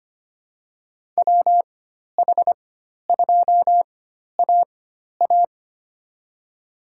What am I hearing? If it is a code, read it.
Morse code, heard as a keyed tone, about 25 wpm, spaced as for 12 wpm, 700 Hz. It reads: WH2AA
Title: USA Call Signs - 5 Characters - 4x Character Spacing - 25wpm